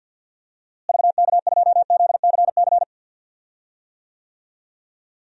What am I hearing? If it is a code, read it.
Morse code, heard as a keyed tone, about 48 wpm, 700 Hz. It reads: VK2ZCC
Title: DXCC-100 Call Signs - 6 Characters - Repeated 2x - 50wpm